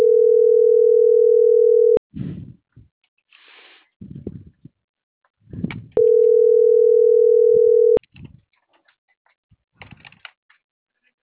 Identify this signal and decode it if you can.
Ringback tone (North America), 440 + 480 Hz.
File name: CENTRAL MABE / out mariel.reyes - .71355